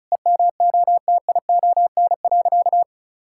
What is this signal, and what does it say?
Morse code, heard as a keyed tone, about 36 wpm, 700 Hz. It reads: EMOTION.
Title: Sentences from Top 1600 Words 35wpm